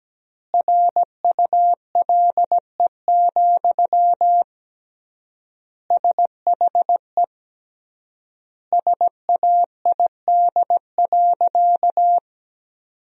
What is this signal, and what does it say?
Morse code, heard as a keyed tone, about 17 wpm, 700 Hz. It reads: RULE, SHE SAID.
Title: 3x Word-Spacing - Sentences from Top 400 Words - 17wpm